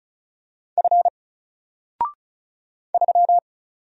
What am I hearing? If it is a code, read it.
Morse code, heard as a keyed tone, about 36 wpm, 700 Hz. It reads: F 3
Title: Single Character (Letter:Number:Punctuation:Prosign) 35wpm